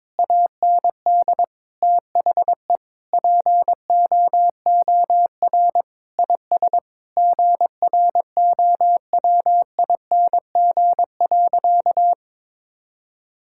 Morse code, heard as a keyed tone, about 22 wpm, 700 Hz. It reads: AND THE POOR IS GROWING.
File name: practice / 22wpm / Sentences with 8-Word Phrases - 22wpm